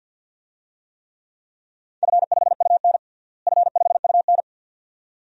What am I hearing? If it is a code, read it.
Morse code, heard as a keyed tone, about 51 wpm, 700 Hz. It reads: F5UN F5UN
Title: DXCC-100 Call Signs - 4 Characters - Repeated 2x - 50wpm